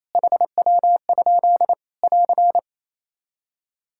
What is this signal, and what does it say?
Morse code, heard as a keyed tone, about 28 wpm, 700 Hz. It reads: HW? <AR>
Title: K1USN SST QSO Practice 28wpm